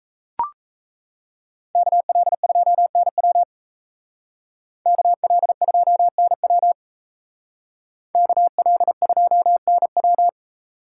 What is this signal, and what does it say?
Morse code, heard as a keyed tone, about 34 wpm, 700 Hz. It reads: KL2NW KL2NW KL2NW
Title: Speed Racing - USA Call Signs 28wpm